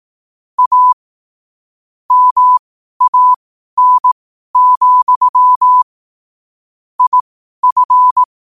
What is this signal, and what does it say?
Morse code, heard as a keyed tone, about 18 wpm, spaced as for 12 wpm, 1000 Hz. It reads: A MAN, IF
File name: qotdfiles / 12wpm